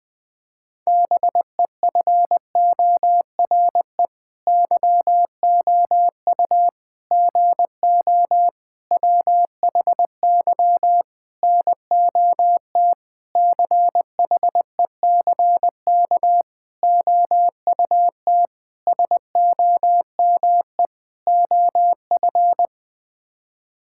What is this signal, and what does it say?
Morse code, heard as a keyed tone, about 21 wpm, 700 Hz. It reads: BEFORE YOU GO WHY NOT CHECK OUT SOME OF
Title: Sets of 9 Words - 20wpm